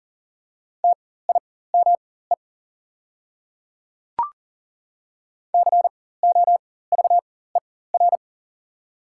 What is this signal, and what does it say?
Morse code, heard as a keyed tone, about 38 wpm, spaced as for 18 wpm, 700 Hz. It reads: TIME COVER
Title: 4x Character Spacing - Top 500 Words - 40wpm